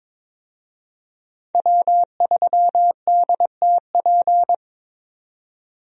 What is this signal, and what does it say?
Morse code, heard as a keyed tone, about 21 wpm, 700 Hz. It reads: W3DTP
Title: USA Call Signs - 5 Characters - 22wpm